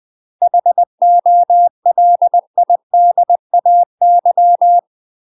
Morse code, heard as a keyed tone, about 20 wpm, 700 Hz. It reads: HOLIDAY
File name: UK MASTER QSOs TRAINING / 20WPM / UK MASTER QSOs 0014